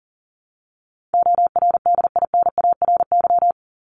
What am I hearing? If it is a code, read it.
Morse code, heard as a keyed tone, about 40 wpm, 700 Hz. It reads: ORDINARY
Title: Top 1751-2000 Words 40wpm